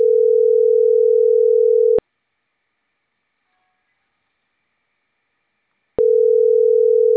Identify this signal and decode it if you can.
Ringback tone (North America), 440 + 480 Hz.